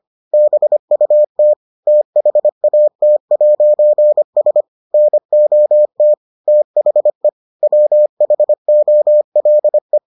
Morse code, heard as a keyed tone, about 25 wpm, 600 Hz. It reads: BUT THAT'S NOT THE WHOLE